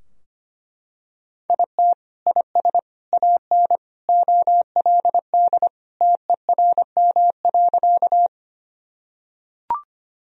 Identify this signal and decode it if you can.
Morse code, heard as a keyed tone, about 25 wpm, 700 Hz. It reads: IT IS AN OLD TERM.
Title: Sentences from Top 1000 Words - Review Part 4 - 25wpm